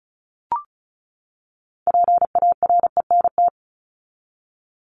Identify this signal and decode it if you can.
Morse code, heard as a keyed tone, about 34 wpm, 700 Hz. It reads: PARENT
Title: ES:S Suffix - Top 100 Words with Base Words 35wpm